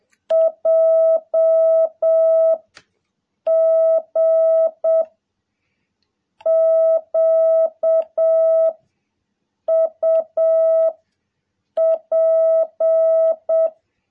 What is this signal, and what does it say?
Morse code, heard as a keyed tone, about 7 wpm, 660 Hz. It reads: J G Q U P
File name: jh8cbh / 2021cw